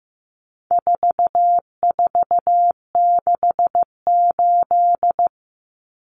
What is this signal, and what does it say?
Morse code, heard as a keyed tone, about 15 wpm, 700 Hz. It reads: 4468
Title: Numbers - 4-Digit 15wpm